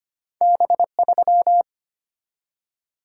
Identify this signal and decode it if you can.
Morse code, heard as a keyed tone, about 25 wpm, 700 Hz. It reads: B3
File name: Two Letter-Number Combinations - 25wpm